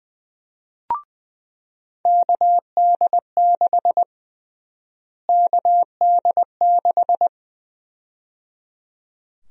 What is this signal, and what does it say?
Morse code, heard as a keyed tone, about 20 wpm, 700 Hz. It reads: KD6 KD6
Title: USA Call Sign Prefixes - Repeated 2x - 20wpm